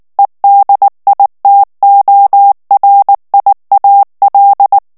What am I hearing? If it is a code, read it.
Morse code, heard as a keyed tone, about 19 wpm, 800 Hz. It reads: EDITORIAL
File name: ANGLAIS texte clair 3 vitesse 18 mots minute